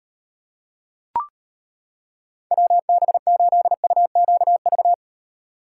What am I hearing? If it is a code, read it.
Morse code, heard as a keyed tone, about 32 wpm, 700 Hz. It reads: WB8UQV